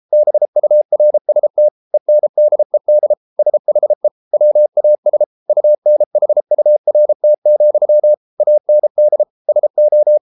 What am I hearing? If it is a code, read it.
Morse code, heard as a keyed tone, about 33 wpm, 600 Hz. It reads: BURST ENDED SHE WAS UNHURT, AND SO